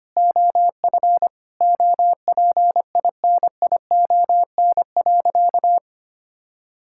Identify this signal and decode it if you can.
Morse code, heard as a keyed tone, about 25 wpm, 700 Hz. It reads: OF OPINION.
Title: Sentences from Top 1600 Words 25wpm